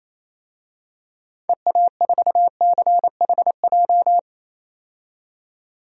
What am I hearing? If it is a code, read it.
Morse code, heard as a keyed tone, about 28 wpm, 700 Hz. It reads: EA4CHJ
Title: DXCC-100 Call Signs - 6 Characters - 28wpm